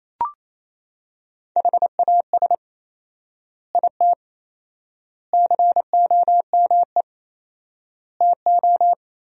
Morse code, heard as a keyed tone, about 28 wpm, 700 Hz. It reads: HAS IT COME TO
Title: 4x Word-Spacing - Sentences from Top 100 Words - 28wpm